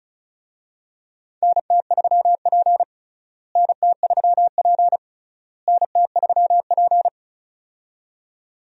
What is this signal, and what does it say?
Morse code, heard as a keyed tone, about 36 wpm, 700 Hz. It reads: NT3P NT3P NT3P
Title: USA Call Signs - 4 Characters - Repeated 3x - 35wpm